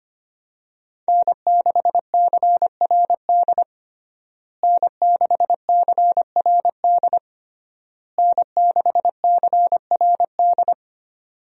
Morse code, heard as a keyed tone, about 25 wpm, 700 Hz. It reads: N6CRD N6CRD N6CRD
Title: USA Call Signs - 5 Characters - Repeated 3x - 25wpm